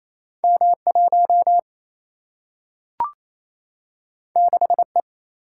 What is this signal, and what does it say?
Morse code, heard as a keyed tone, about 28 wpm, 700 Hz. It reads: M1 6E
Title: Single-Letter Single-Number 28wpm